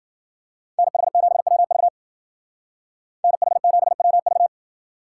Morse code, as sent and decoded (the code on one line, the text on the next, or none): -. .... --... .-- ...- / -. .... --... .-- ...-
NH7WV NH7WV